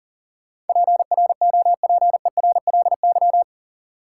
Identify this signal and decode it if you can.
Morse code, heard as a keyed tone, about 40 wpm, 700 Hz. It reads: PROPERLY
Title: Sentences from Top 1100 Words - New Word Drill - 40wpm